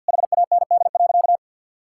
Morse code, heard as a keyed tone, about 52 wpm, 690 Hz. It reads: HAND.
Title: Sentences from Top 1000 Words - Review Part 5 - 50wpm